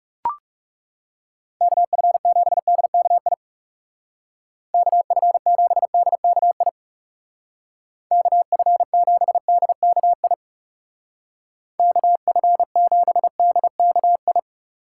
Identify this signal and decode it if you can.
Morse code, heard as a keyed tone, about 36 wpm, 700 Hz. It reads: KF7DKI KF7DKI KF7DKI KF7DKI
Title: Speed Racing - USA 2x3 Call Signs 30wpm